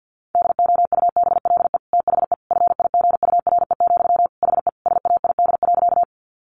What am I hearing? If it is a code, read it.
Morse code, heard as a keyed tone, about 48 wpm, 700 Hz. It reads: DOUBLE THE FIGURE, HE SAID.
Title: Sentences from Top 1400 Words 50wpm